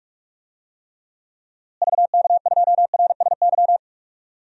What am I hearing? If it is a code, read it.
Morse code, heard as a keyed tone, about 46 wpm, 700 Hz. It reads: VK2RSY
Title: DXCC-100 Call Signs - 6 Characters - Repeated 2x - 45wpm